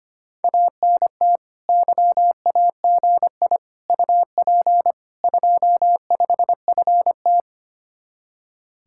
Morse code, heard as a keyed tone, about 32 wpm, 700 Hz. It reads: ANT YAGI UP 25FT